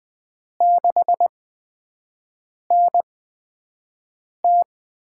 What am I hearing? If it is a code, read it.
Morse code, heard as a keyed tone, about 20 wpm, spaced as for 5 wpm, 700 Hz. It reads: 6NT
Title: USA Call Signs - 8x Character Spacing - 20wpm